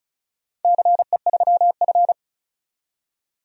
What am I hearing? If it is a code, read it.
Morse code, heard as a keyed tone, about 36 wpm, 690 Hz. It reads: CE3F
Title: DXCC-100 Call Signs - 4 Characters - 35wpm